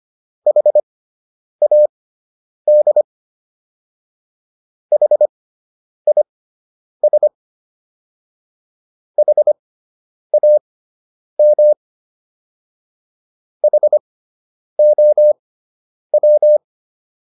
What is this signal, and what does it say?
Morse code, heard as a keyed tone, about 25 wpm, spaced as for 9 wpm, 600 Hz. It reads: HAD HIS HAM HOW